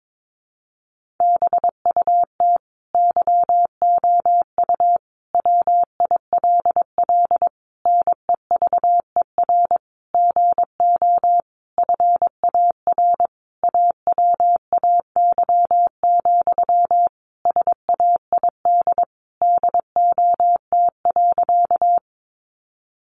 Morse code, heard as a keyed tone, about 22 wpm, 700 Hz. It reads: BUT YOU WILL NEVER GO FAR AWAY, SAID DOT.
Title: Sentences from Top 1500 Words 22wpm